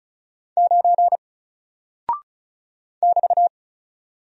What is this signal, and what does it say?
Morse code, heard as a keyed tone, about 36 wpm, 700 Hz. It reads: 9 <BT>
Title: Single Character (Letter:Number:Punctuation:Prosign) 35wpm